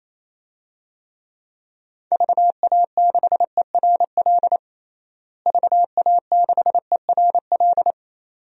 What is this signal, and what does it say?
Morse code, heard as a keyed tone, about 28 wpm, 700 Hz. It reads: VA6ERL VA6ERL
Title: DXCC-100 Call Signs - 6 Characters - Repeated 2x - 28wpm